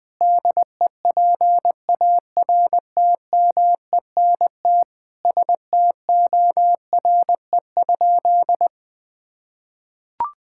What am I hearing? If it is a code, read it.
Morse code, heard as a keyed tone, about 20 wpm, 700 Hz. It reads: DEPARTMENT STORE?
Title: Sentences from Top 1000 Words - Review Part 5 - 20wpm